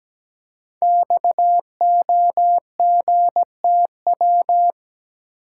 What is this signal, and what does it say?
Morse code, heard as a keyed tone, about 17 wpm, 700 Hz. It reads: XOGTW
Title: Code Groups - 5 Letter:Numbers - 17wpm